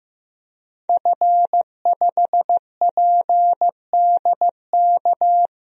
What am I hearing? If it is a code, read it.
Morse code, heard as a keyed tone, about 15 wpm, 700 Hz. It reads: F5PDK
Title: DXCC-100 Call Signs - 5 Characters - Repeated 3x - 15wpm